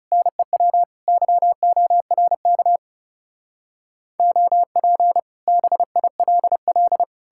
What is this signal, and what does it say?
Morse code, heard as a keyed tone, about 38 wpm, 700 Hz. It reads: NEW YORK OP BILL